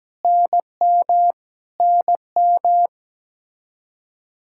Morse code, heard as a keyed tone, about 17 wpm, 700 Hz. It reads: NM NM
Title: Practice QSO - Exchange 1 - Beginning - 17wpm